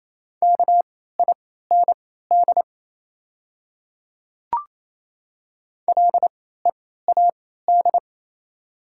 Morse code, heard as a keyed tone, about 28 wpm, spaced as for 16 wpm, 700 Hz. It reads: KIND LEAD
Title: 3x Character Spacing - Top 500 Words - 28wpm